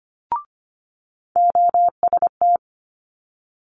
Morse code, heard as a keyed tone, about 25 wpm, 700 Hz. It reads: OST
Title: Multi-letter Phonemes Quick Fire 25wpm